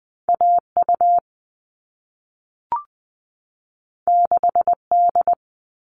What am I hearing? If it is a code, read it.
Morse code, heard as a keyed tone, about 20 wpm, 700 Hz. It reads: AU 6D
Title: All Cut Numbers - 2-Digit 20wpm